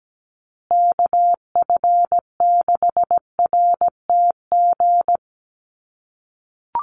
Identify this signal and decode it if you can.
Morse code, heard as a keyed tone, about 17 wpm, 700 Hz. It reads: KF6RTG
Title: USA Call Signs Repeated 3x 17wpm